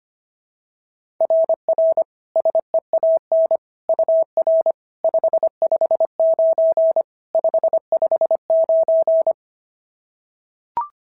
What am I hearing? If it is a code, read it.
Morse code, heard as a keyed tone, about 25 wpm, 690 Hz. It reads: RR SEAN UR 559 559
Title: SOTA QSO Practice 25wpm